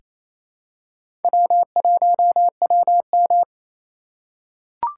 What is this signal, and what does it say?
Morse code, heard as a keyed tone, about 28 wpm, 700 Hz. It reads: W1WM
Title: USA Call Signs - 4 Characters - 4x Character Spacing - 28wpm